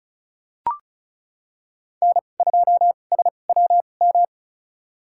Morse code, heard as a keyed tone, about 36 wpm, spaced as for 25 wpm, 700 Hz. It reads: N2SWM
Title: USA Call Signs - 5 Characters - 2x Character Spacing - 35wpm